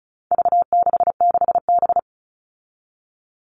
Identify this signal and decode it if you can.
Morse code, heard as a keyed tone, about 34 wpm, 700 Hz. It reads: V66B
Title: All Cut Numbers - 4-Digit 35wpm